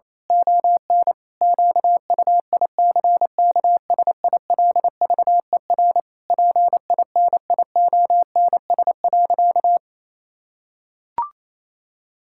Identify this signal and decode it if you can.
Morse code, heard as a keyed tone, about 28 wpm, 700 Hz. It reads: ON QUICKSILVER PINIONS.
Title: Pangram Challenge v2 - 28wpm